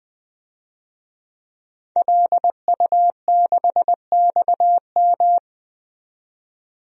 Morse code, heard as a keyed tone, about 20 wpm, 700 Hz. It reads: LU6XM
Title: DXCC-100 Call Signs - 5 Characters - 20wpm